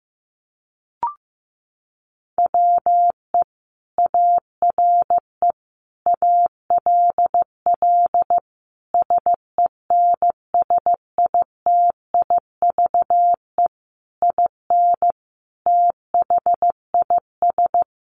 Morse code, heard as a keyed tone, about 15 wpm, 700 Hz. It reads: WE ARE ALL SENSITIVE IN THIS
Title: Sentences from Top 1400 Words 15wpm